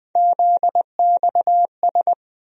Morse code, heard as a keyed tone, about 20 wpm, 700 Hz. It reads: ZXS
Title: USA Call Sign Postfixes - 3 Letters 20wpm